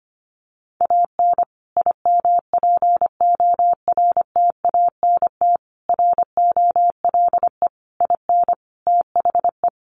Morse code, heard as a keyed tone, about 25 wpm, 700 Hz. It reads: AN IMPORTANT ROLE IN THE
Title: Sets of 5 Words Encore2 25wpm